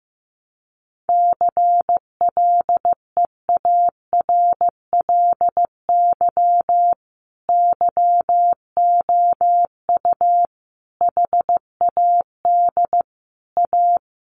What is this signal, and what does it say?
Morse code, heard as a keyed tone, about 15 wpm, 700 Hz. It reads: CLEARLY YOU HAD A
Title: Sentences from Top 1300 Words 15wpm